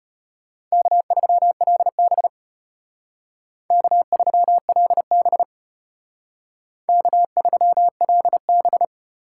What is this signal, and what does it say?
Morse code, heard as a keyed tone, about 33 wpm, 700 Hz. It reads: K3LB K3LB K3LB